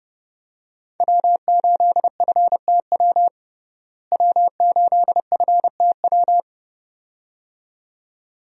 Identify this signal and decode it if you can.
Morse code, heard as a keyed tone, about 32 wpm, 700 Hz. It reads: W8FTW W8FTW
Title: USA Call Signs - 5 Characters - Repeated 2x - 30wpm